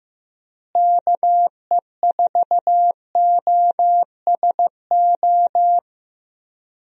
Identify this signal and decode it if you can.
Morse code, heard as a keyed tone, about 15 wpm, 700 Hz. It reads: KE4OSO
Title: USA Call Signs - 6 Characters - Repeated 3x - 15wpm